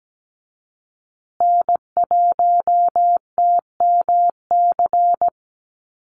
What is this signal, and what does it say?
Morse code, heard as a keyed tone, about 17 wpm, 700 Hz. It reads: N1TMC
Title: USA Call Signs Repeated 2x 17wpm